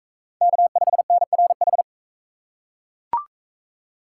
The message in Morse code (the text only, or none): K5NRH